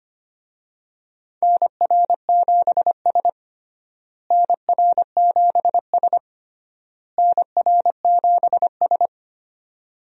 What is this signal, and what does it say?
Morse code, heard as a keyed tone, about 25 wpm, 700 Hz. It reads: NR7S NR7S NR7S
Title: USA Call Signs - 4 Characters - Repeated 3x - 25wpm